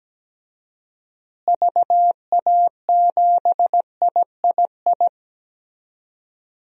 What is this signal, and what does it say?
Morse code, heard as a keyed tone, about 17 wpm, 700 Hz. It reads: VA7III
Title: DXCC-100 Call Signs - 6 Characters - Repeated 3x - 17wpm